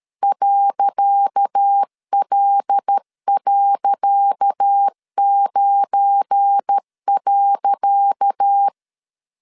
Morse code, heard as a keyed tone, about 13 wpm, 800 Hz. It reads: .L.9.
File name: LEZIONI / 10 SIGNS / 10C DOT 12wpm 2451